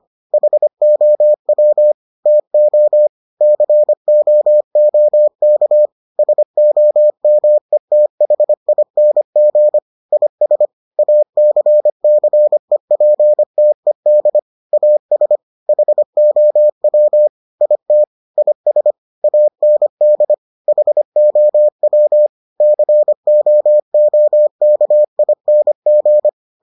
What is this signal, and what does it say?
Morse code, heard as a keyed tone, about 25 wpm, 600 Hz. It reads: HOW TO COOK SOMETHING IS ACCEPTED AS HOW IT IS AND HOW COOKING